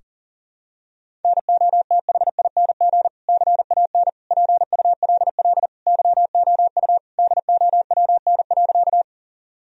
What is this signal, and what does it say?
Morse code, heard as a keyed tone, about 40 wpm, 700 Hz. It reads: NOTHING CAN PULL YOU DOWN.
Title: Sentences from Top 1000 Words - Review Part 2 - 40wpm